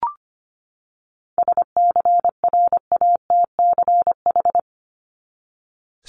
Morse code, heard as a keyed tone, about 25 wpm, 700 Hz. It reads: SCRATCH
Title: Top 1501-1750 Words 25wpm